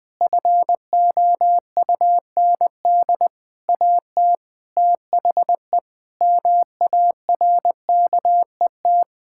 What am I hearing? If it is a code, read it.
Morse code, heard as a keyed tone, about 20 wpm, 700 Hz. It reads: FOUND AT THE MARKET
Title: Sentences from Top 1000 Words - Review Part 6 - 20wpm